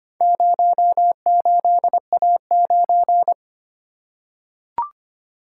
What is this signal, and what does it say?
Morse code, heard as a keyed tone, about 25 wpm, 700 Hz. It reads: 08A9
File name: practice / 25wpm / Four Letter-Number Combinations - 25wpm